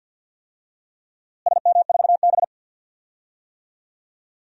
Morse code, heard as a keyed tone, about 46 wpm, 700 Hz. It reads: SM4B
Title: DXCC-100 Call Signs - 4 Characters - Repeated 3x - 50wpm